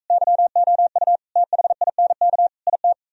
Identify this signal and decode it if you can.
Morse code, heard as a keyed tone, about 42 wpm, 700 Hz. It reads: YOU THINK IT